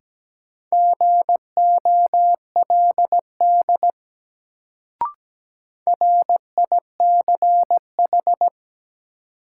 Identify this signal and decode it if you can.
Morse code, heard as a keyed tone, about 17 wpm, 700 Hz. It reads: GOLD RICH
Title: Sentences from Top 1100 Words - New Word Drill Alt - 17wpm